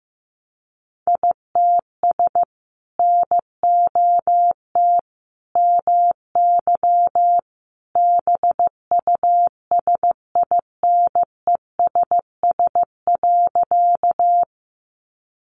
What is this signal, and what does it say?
Morse code, heard as a keyed tone, about 15 wpm, 700 Hz. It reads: ITS NOT MY BUSINESS.
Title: Sentences from Top 600 Words 15wpm